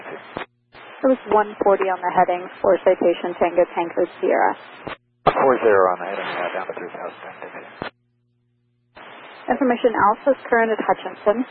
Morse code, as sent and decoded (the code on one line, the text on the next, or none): none